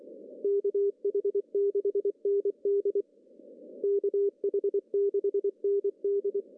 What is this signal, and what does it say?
Morse code, heard as a keyed tone, about 23 wpm, 400 Hz. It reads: KH6ND KH6ND
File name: CW / cqww-cw-k5tr